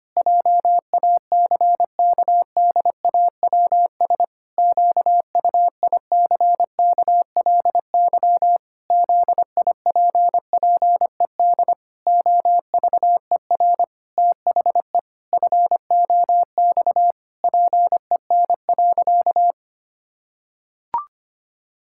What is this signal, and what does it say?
Morse code, heard as a keyed tone, about 25 wpm, 700 Hz. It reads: JACKDAWS QUICKLY ZIPPED OVER THE FOX PEN.
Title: Pangram Challenge v2 - 25wpm